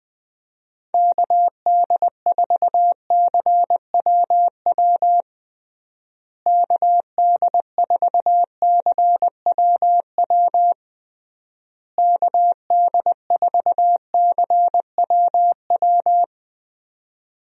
Morse code, heard as a keyed tone, about 21 wpm, 700 Hz. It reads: KD4CWW KD4CWW KD4CWW
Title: USA Call Signs - 6 Characters - Repeated 3x - 20wpm